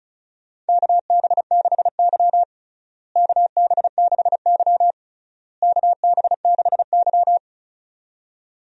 Morse code, heard as a keyed tone, about 36 wpm, 700 Hz. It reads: KB6Y KB6Y KB6Y
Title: USA Call Signs - 4 Characters - Repeated 3x - 35wpm